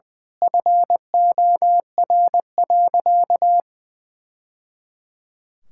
Morse code, heard as a keyed tone, about 21 wpm, 700 Hz. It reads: FOR.
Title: Sentences from Top 1000 Words - Review Part 6 - 20wpm